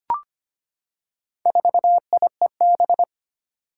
Morse code, heard as a keyed tone, about 25 wpm, 700 Hz. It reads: 4IEB
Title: Four Challenging Letter-Number Combinations 25wpm